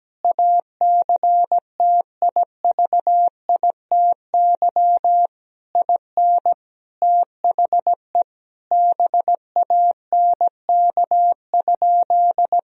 Morse code, heard as a keyed tone, about 17 wpm, 700 Hz. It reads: ACTIVITY IN THE BANK?
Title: Sentences from Top 1000 Words - Review Part 4 - 17wpm